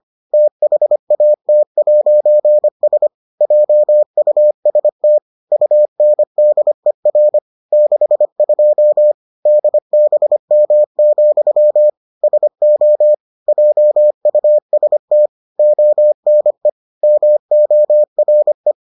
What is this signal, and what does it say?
Morse code, heard as a keyed tone, about 25 wpm, 600 Hz. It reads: THAT'S JUST UNDER 62 DBM, SO JUST ONE MORE